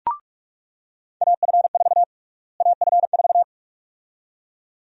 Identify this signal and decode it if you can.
Morse code, heard as a keyed tone, about 46 wpm, 700 Hz. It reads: AF4 AF4
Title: USA Call Sign Prefixes - Repeated 2x - 45wpm